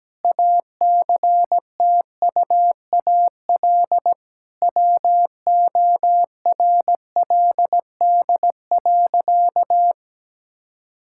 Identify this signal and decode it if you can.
Morse code, heard as a keyed tone, about 17 wpm, 700 Hz. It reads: ACTUAL WORLD.
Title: Sentences from Top 1000 Words - Review Part 5 - 17wpm